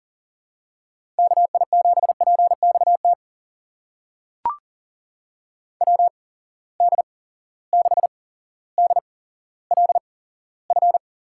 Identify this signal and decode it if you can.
Morse code, heard as a keyed tone, about 40 wpm, spaced as for 17 wpm, 700 Hz. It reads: KI7PXT WD6DLF
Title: USA Call Signs - 6 Characters - 8x Character Spacing - 40wpm